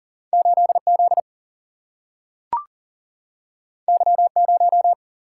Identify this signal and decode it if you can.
Morse code, heard as a keyed tone, about 40 wpm, 700 Hz. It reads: 8Z Y0
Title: Single-Letter Single-Number 40wpm